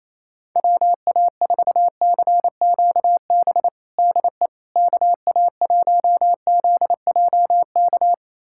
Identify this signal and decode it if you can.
Morse code, heard as a keyed tone, about 28 wpm, 700 Hz. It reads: WA4CQB DE KA1ZJK
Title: USA Callsign DE Callsign - 6 Characters - 28wpm